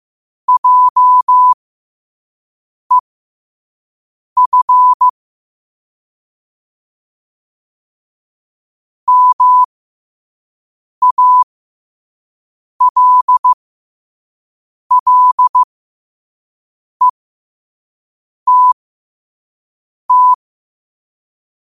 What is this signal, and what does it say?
Morse code, heard as a keyed tone, about 15 wpm, spaced as for 5 wpm, 1000 Hz. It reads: JEF MALLETT
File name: qotdfiles / 5wpm